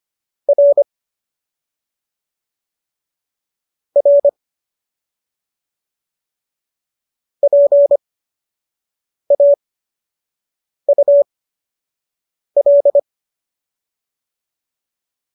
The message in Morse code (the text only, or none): R R PAUL